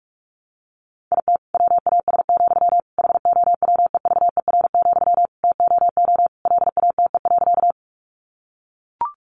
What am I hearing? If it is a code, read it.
Morse code, heard as a keyed tone, about 44 wpm, 700 Hz. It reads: IT WAS, HOWEVER, TOO LATE.
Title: Sentences from Top 600 Words 45wpm